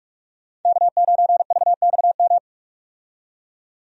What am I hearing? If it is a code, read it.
Morse code, heard as a keyed tone, about 43 wpm, 700 Hz. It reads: K9VXM
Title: USA Call Signs - 5 Characters - 45wpm